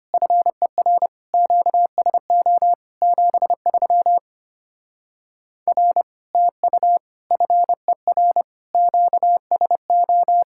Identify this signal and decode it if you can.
Morse code, heard as a keyed tone, about 33 wpm, 700 Hz. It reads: FER QSO 73 R TU FER QSO